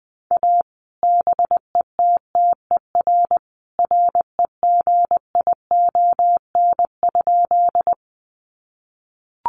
Morse code, heard as a keyed tone, about 20 wpm, 700 Hz. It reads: A BETTER REGION?
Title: Sentences from Top 1500 Words 20wpm